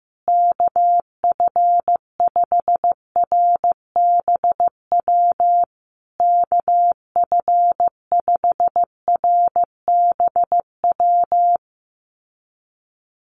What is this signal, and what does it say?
Morse code, heard as a keyed tone, about 15 wpm, 700 Hz. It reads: KF5RBW KF5RBW
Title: USA Call Signs Repeated 3x 15wpm